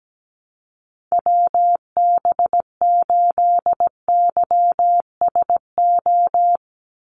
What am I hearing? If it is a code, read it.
Morse code, heard as a keyed tone, about 17 wpm, 700 Hz. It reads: WB8YSO
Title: USA Call Signs 17wpm